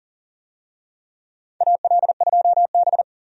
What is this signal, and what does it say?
Morse code, heard as a keyed tone, about 40 wpm, 700 Hz. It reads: AL2B
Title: USA Call Signs - 4 Characters - 40wpm